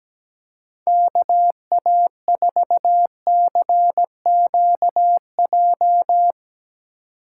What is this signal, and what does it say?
Morse code, heard as a keyed tone, about 19 wpm, 700 Hz. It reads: KA4CQJ
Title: Speed Racing - USA 2x3 Call Signs 17wpm